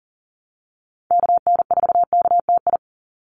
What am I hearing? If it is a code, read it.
Morse code, heard as a keyed tone, about 40 wpm, 700 Hz. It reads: KN4KTI
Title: USA Call Signs 40wpm